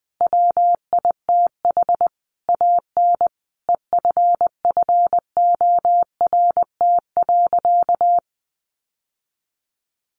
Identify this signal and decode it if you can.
Morse code, heard as a keyed tone, about 20 wpm, 700 Hz. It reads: WITH AN EFFORT.
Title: Sentences from Top 1500 Words 20wpm